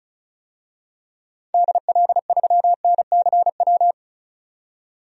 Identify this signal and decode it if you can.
Morse code, heard as a keyed tone, about 36 wpm, 700 Hz. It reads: DL3NCW
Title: DXCC-100 Call Signs - 6 Characters - 35wpm